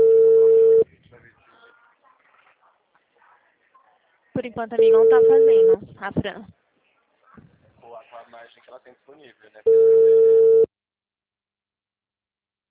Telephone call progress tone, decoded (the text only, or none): ringback tone